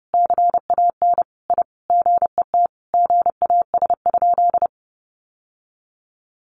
Morse code, heard as a keyed tone, about 28 wpm, 700 Hz. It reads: CAN I GET GAS?
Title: Sentences from Top 1100 Words 30wpm